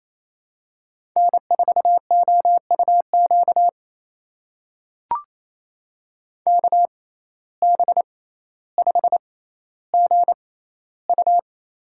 Morse code, heard as a keyed tone, about 28 wpm, spaced as for 14 wpm, 700 Hz. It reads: N4OUQ KB5GU
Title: USA Call Signs - 5 Characters - 6x Character Spacing - 28wpm